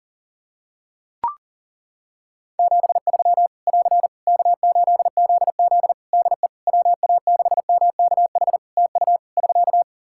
Morse code, heard as a keyed tone, about 40 wpm, 700 Hz. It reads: 73 <AR> K8ZZ DE WA6MKH TU <SK>
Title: K1USN SST QSO Practice 40wpm